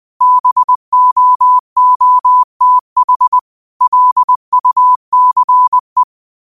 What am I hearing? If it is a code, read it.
Morse code, heard as a keyed tone, about 20 wpm, 1000 Hz. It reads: BOOTH LUCE
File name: qotdfiles / 20wpm